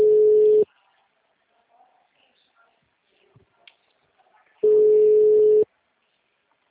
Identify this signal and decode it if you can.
Ringback tone (Europe, 425 Hz).